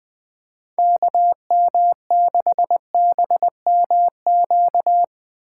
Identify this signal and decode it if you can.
Morse code, heard as a keyed tone, about 20 wpm, 700 Hz. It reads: KM6BMQ
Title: USA Call Signs - 6 Characters - 3x Character Spacing - 20wpm